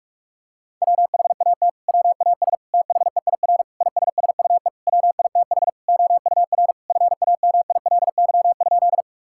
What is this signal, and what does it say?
Morse code, heard as a keyed tone, about 46 wpm, 700 Hz. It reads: WHAT WAS THEIR ISSUE WITH OUR FAMILY?
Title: Sentences from Top 1000 Words - Review Part 6 - 45wpm